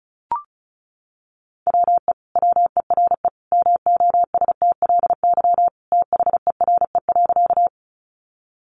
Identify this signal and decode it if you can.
Morse code, heard as a keyed tone, about 34 wpm, 700 Hz. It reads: WE WERE MOSTLY THERE.
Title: Sentences from Top 1300 Words 35wpm